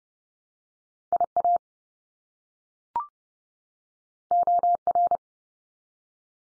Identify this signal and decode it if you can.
Morse code, heard as a keyed tone, about 30 wpm, 700 Hz. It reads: IA OR
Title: US States 30WPM